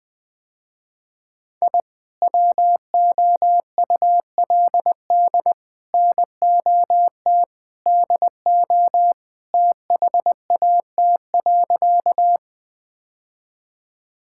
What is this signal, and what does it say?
Morse code, heard as a keyed tone, about 20 wpm, 700 Hz. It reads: I WOULD NOT DO THAT.
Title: Sentences with 3-Word Phrases - Spoken Beginning - 20wpm